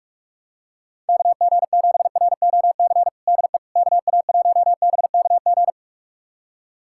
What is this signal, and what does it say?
Morse code, heard as a keyed tone, about 43 wpm, 700 Hz. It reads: KG7ROC DE KA1BKG
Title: USA Callsign DE Callsign - 6 Characters - 45wpm